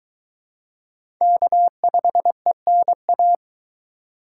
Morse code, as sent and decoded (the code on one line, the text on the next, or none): -.- ..... -. .-
K5ENA